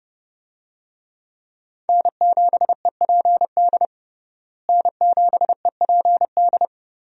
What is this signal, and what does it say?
Morse code, heard as a keyed tone, about 30 wpm, 700 Hz. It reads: N7EPD N7EPD
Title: USA Call Signs - 5 Characters - Repeated 2x - 30wpm